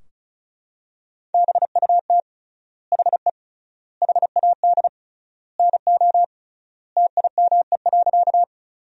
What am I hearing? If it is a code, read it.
Morse code, heard as a keyed tone, about 36 wpm, 700 Hz. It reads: BUT HE HAD NO TIME.
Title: 3x Word-Spacing - Sentences from Top 100 Words - 35wpm